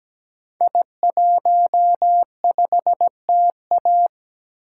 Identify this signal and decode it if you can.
Morse code, heard as a keyed tone, about 17 wpm, 700 Hz. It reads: I15TA
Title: Code Groups - 5 Letter:Numbers - 17wpm